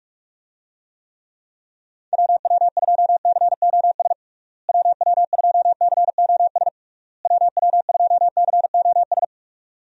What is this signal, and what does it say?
Morse code, heard as a keyed tone, about 46 wpm, 700 Hz. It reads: WW2COS WW2COS WW2COS
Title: USA Call Signs - 6 Characters - Repeated 3x - 45wpm